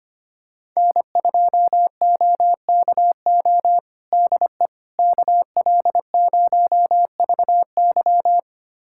Morse code, heard as a keyed tone, about 25 wpm, 700 Hz. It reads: N2OKO DE KL0VY
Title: USA Callsign DE Callsign 25wpm